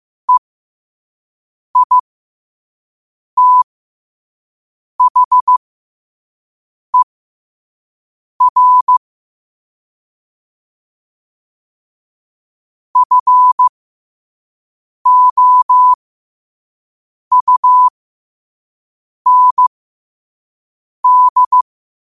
Morse code, heard as a keyed tone, about 15 wpm, spaced as for 5 wpm, 1000 Hz. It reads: EITHER FOUND
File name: qotdfiles / 5wpm